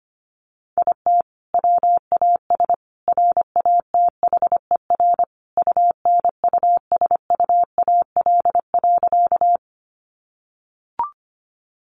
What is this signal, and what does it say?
Morse code, heard as a keyed tone, about 25 wpm, 700 Hz. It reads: IT WAS RATHER UNUSUAL.
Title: Sentences from Top 1400 Words 25wpm